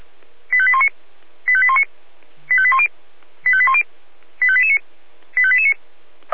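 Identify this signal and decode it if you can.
Five-tone selective call (ZVEI), heard as quick runs of five tones, tones of about 70 ms. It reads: 85819 85819 85810 85810 85889 85889
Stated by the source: ZVEI five-tone selective call 85320, received on SDR.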